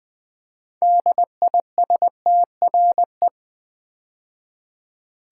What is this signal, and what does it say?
Morse code, heard as a keyed tone, about 20 wpm, 700 Hz. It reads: DISTRE
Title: Build A Word - 2 Letters At A Time - 8-Letter Words - 20wpm